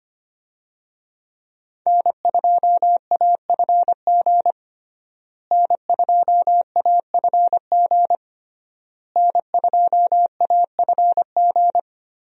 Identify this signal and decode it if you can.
Morse code, heard as a keyed tone, about 25 wpm, 700 Hz. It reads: N2AFG N2AFG N2AFG
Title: USA Call Signs - 5 Characters - Repeated 3x - 25wpm